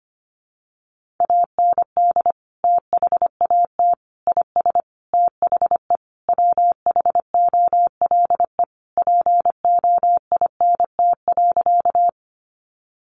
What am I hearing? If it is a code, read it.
Morse code, heard as a keyed tone, about 25 wpm, 700 Hz. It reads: AND THAT IS THE WHOLE POINT.
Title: Sentences from Top 400 Words 25wpm